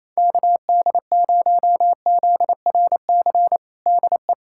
Morse code, heard as a keyed tone, about 28 wpm, 700 Hz. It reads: KD0ZRC DE
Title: Practice QSO - Exchange 4 - Ending - 28wpm